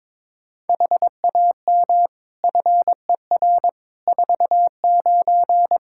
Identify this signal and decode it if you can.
Morse code, heard as a keyed tone, about 22 wpm, 700 Hz. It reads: HAM FER 49
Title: K1USN SST QSO Practice 22wpm